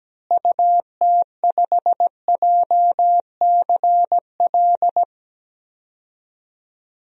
Morse code, heard as a keyed tone, about 17 wpm, 700 Hz. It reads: UT5JCL
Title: DXCC-100 Call Signs - 6 Characters - Repeated 2x - 17wpm